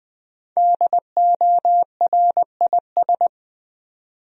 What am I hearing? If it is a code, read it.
Morse code, heard as a keyed tone, about 20 wpm, 700 Hz. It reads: DORIS
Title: USA Common Names - Max 5 Letters - 20wpm